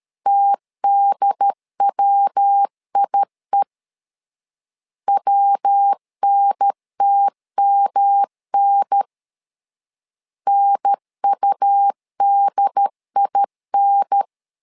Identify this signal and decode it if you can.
Morse code, heard as a keyed tone, about 13 wpm, 800 Hz. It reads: TDWIE WNTMN NUDIN
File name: LEZIONI / 3 MWRO / 3 MW 12wpm 1633